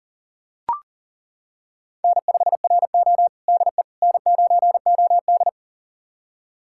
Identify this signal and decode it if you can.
Morse code, heard as a keyed tone, about 37 wpm, 700 Hz. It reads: N5RO DE N9OD
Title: USA Callsign DE Callsign - 4 Characters - 40wpm